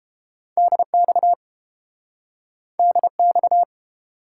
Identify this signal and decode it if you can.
Morse code, heard as a keyed tone, about 32 wpm, 700 Hz. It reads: DX DX